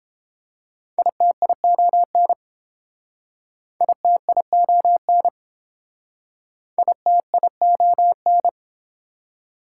Morse code, heard as a keyed tone, about 30 wpm, 700 Hz. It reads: ITION ITION ITION